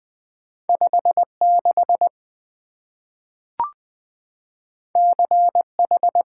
Morse code, heard as a keyed tone, about 20 wpm, 700 Hz. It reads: 56 CH
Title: Two Letter-Number Combinations - 20wpm